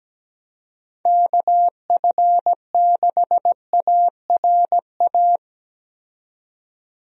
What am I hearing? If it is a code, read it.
Morse code, heard as a keyed tone, about 17 wpm, 700 Hz. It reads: KF6ARA
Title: USA Call Signs - 6 Characters - 17wpm